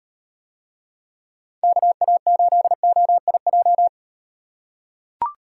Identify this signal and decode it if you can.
Morse code, heard as a keyed tone, about 38 wpm, 700 Hz. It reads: KA8OIJ